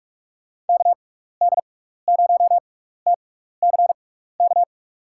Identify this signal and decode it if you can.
Morse code, heard as a keyed tone, about 44 wpm, spaced as for 15 wpm, 700 Hz. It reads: KD0TCK
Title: USA Call Signs - 6 Characters - 6x Character Spacing - 45wpm